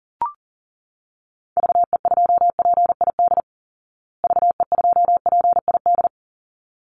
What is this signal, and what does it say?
Morse code, heard as a keyed tone, about 40 wpm, 700 Hz. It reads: VE2PID VE2PID
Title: Call Signs Repeated 3x - 40wpm